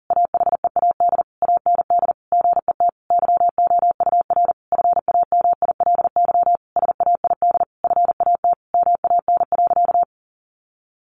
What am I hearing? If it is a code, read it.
Morse code, heard as a keyed tone, about 40 wpm, 700 Hz. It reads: AHEAD AND GET YOUR FAMILY SAID FAT MAN.
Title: Sentences from Top 1100 Words 40wpm